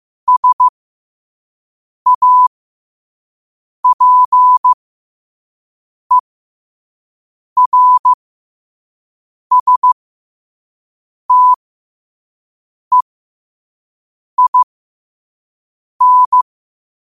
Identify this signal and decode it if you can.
Morse code, heard as a keyed tone, about 15 wpm, spaced as for 5 wpm, 1000 Hz. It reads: SAPERSTEIN
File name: qotdfiles / 5wpm